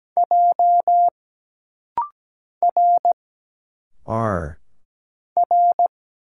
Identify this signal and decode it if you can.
Morse code, heard as a keyed tone, about 17 wpm, 700 Hz. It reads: J R R
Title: Single Letter-Number Alt - Rapid-Fire 17wpm